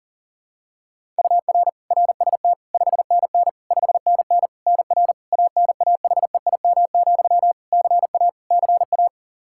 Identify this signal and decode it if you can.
Morse code, heard as a keyed tone, about 40 wpm, 700 Hz. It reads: UR RST 5NN 5NN NR ANAHEIM, CA CA
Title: Practice QSO - Exchange 1 - Beginning - 40wpm